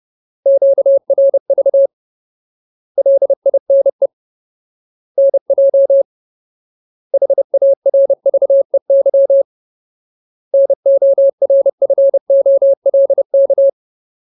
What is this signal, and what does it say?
Morse code, heard as a keyed tone, about 30 wpm, 550 Hz. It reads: QRV LINE NJ HARVEY NORFOLK
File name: Ultimate Word List / 30WPM Ultimate